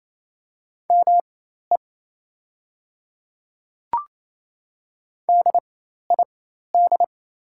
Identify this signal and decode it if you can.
Morse code, heard as a keyed tone, about 27 wpm, spaced as for 13 wpm, 700 Hz. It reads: ME DID
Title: 4x Character Spacing - Top 200 Words - 28wpm